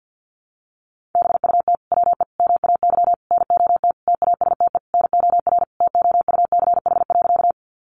Morse code, heard as a keyed tone, about 49 wpm, 700 Hz. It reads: BUT WE MAY NOT TASTE NOR TOUCH.
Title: Sentences from Top 1500 Words 50wpm